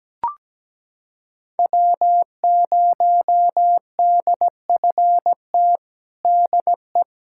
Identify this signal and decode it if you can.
Morse code, heard as a keyed tone, about 17 wpm, 700 Hz. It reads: W0DFT DE
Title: USA Callsign DE Callsign - 5 Characters - 17wpm